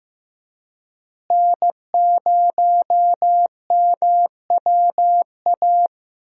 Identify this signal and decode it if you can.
Morse code, heard as a keyed tone, about 15 wpm, 700 Hz. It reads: N0MWA
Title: USA Call Signs - 5 Characters - Repeated 2x - 15wpm